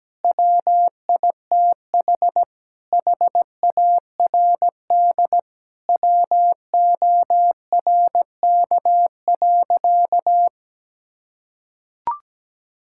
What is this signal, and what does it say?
Morse code, heard as a keyed tone, about 17 wpm, 700 Hz. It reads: WITH HARD WORK.
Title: Sentences from Top 1000 Words - Review Part 5 - 17wpm